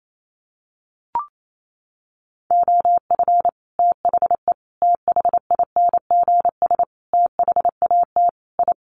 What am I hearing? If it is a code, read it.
Morse code, heard as a keyed tone, about 28 wpm, 690 Hz. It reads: OF THE THINGS THAT I
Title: Sets of 5 Words Encore 28wpm